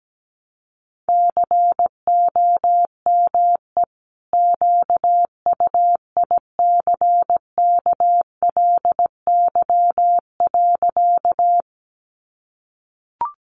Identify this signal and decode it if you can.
Morse code, heard as a keyed tone, about 17 wpm, 700 Hz. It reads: COME QUICKLY.
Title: Sentences from Top 700 Words 17wpm